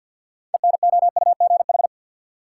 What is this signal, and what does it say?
Morse code, heard as a keyed tone, about 51 wpm, 700 Hz. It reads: ENOUGH
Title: Sets of 2 Words - New Word Drill - 50wpm